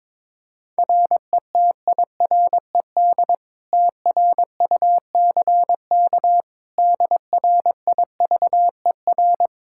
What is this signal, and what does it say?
Morse code, heard as a keyed tone, about 22 wpm, 700 Hz. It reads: RETIRED TRUCK DRIVER
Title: Practice QSO - Exchange 3 - More Info - 22wpm